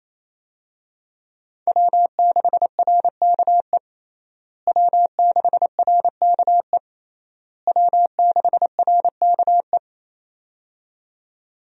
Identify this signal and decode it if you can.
Morse code, heard as a keyed tone, about 28 wpm, 700 Hz. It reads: W6RKE W6RKE W6RKE
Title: USA Call Signs - 5 Characters - Repeated 3x - 28wpm